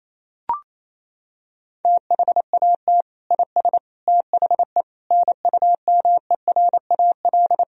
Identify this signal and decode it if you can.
Morse code, heard as a keyed tone, about 28 wpm, 700 Hz. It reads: THAT IS THE NUMERAL
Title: Sentences from Top 1000 Words - Review Part 2 - 28wpm